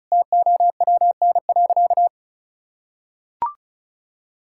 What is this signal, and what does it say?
Morse code, heard as a keyed tone, about 36 wpm, 700 Hz. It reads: TOWN.
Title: Sentences from Top 1600 Words 35wpm